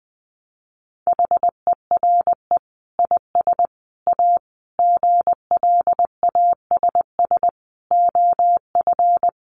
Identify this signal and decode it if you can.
Morse code, heard as a keyed tone, about 20 wpm, 700 Hz. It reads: HERE IS A GLASS OF
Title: Sentences from Top 1300 Words 20wpm